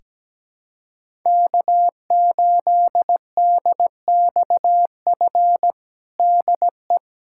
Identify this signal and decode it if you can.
Morse code, heard as a keyed tone, about 17 wpm, 700 Hz. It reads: K8DXF DE
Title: USA Callsign DE Callsign - 5 Characters - 17wpm